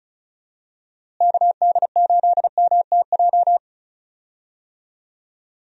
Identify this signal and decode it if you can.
Morse code, heard as a keyed tone, about 36 wpm, 700 Hz. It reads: KD8MTJ
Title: USA Call Signs - 6 Characters - Repeated 3x - 35wpm